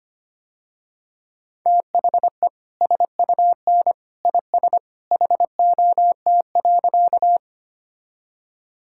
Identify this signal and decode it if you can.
Morse code, heard as a keyed tone, about 25 wpm, 700 Hz. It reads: THE SUN IS HOT.
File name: Easy Sentences with Hints - 25wpm